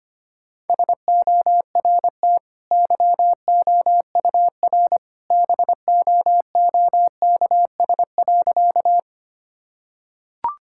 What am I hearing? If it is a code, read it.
Morse code, heard as a keyed tone, about 25 wpm, 700 Hz. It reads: SORT YOUR BOOKS.
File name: Easy Sentences with Hints - 25wpm